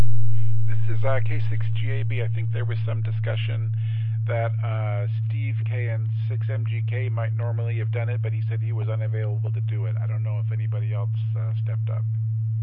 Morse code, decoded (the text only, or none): none